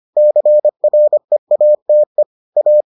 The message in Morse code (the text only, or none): CREATE A